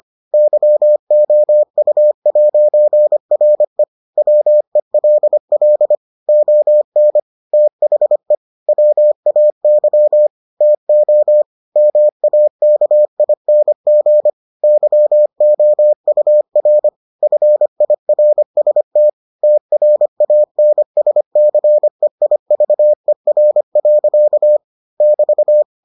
Morse code, heard as a keyed tone, about 25 wpm, 600 Hz. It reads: YOU'RE WELL ON THE WAY TO MAKING YOUR FIRST TRANSCEIVER. <BT>